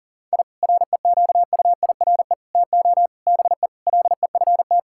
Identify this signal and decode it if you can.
Morse code, heard as a keyed tone, about 40 wpm, 700 Hz. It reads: I REQUIRE TO BE LEFT
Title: Sentences from Top 1000 Words - Review Part 6 - 40wpm